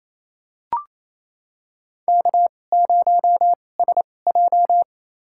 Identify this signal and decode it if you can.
Morse code, heard as a keyed tone, about 27 wpm, spaced as for 20 wpm, 700 Hz. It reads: K0SJ
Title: USA Call Signs - 4 Characters - 2x Character Spacing - 28wpm